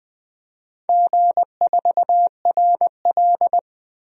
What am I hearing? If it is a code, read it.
Morse code, heard as a keyed tone, about 21 wpm, 700 Hz. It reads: G4RL
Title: DXCC-100 Call Signs - 4 Characters - Repeated 3x - 20wpm